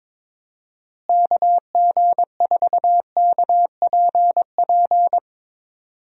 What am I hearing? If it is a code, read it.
Morse code, heard as a keyed tone, about 22 wpm, 700 Hz. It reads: KG4KPP
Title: USA Call Signs - 6 Characters - Repeated 3x - 22wpm